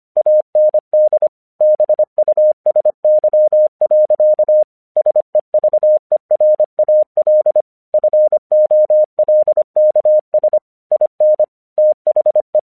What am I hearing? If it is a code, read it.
Morse code, heard as a keyed tone, about 25 wpm, 610 Hz. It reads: AND BUSY. SEVERAL FOLKS IN THE
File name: 1 English 25wpm / AQC05